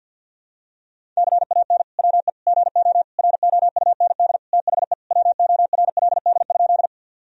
Morse code, heard as a keyed tone, about 49 wpm, 700 Hz. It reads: CAN WE GO ROUND THE WORLD?
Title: Sentences from Top 1000 Words - Review Part 1 - 50wpm